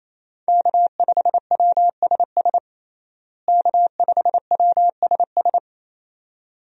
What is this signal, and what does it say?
Morse code, heard as a keyed tone, about 28 wpm, 700 Hz. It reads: K5WSS K5WSS
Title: USA Call Signs - 5 Characters - Repeated 2x - 28wpm